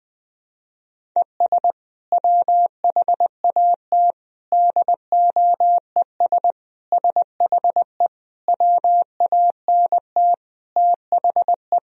Morse code, heard as a keyed tone, about 20 wpm, 700 Hz. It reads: ES WHAT DOES SHE WANT THE
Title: Sentences from Top 400 Words and Abbreviations 20wpm